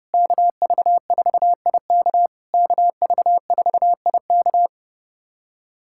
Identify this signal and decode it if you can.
Morse code, heard as a keyed tone, about 30 wpm, 660 Hz. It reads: KV4IK KV4IK
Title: SOTA QSO Practice 30wpm